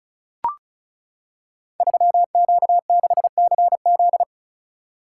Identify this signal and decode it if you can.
Morse code, heard as a keyed tone, about 36 wpm, 700 Hz. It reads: 3Q6CZ
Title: Code Groups - 5 Letter:Numbers - 35wpm